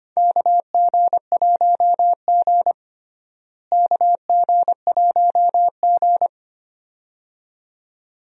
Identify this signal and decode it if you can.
Morse code, heard as a keyed tone, about 25 wpm, 700 Hz. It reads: KG1G KG1G
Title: USA Call Signs - 4 Characters - Repeated 2x - 25wpm